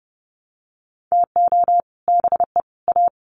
Morse code, heard as a keyed tone, about 30 wpm, 700 Hz. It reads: TO BE A
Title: Sets of 3 Words 30wpm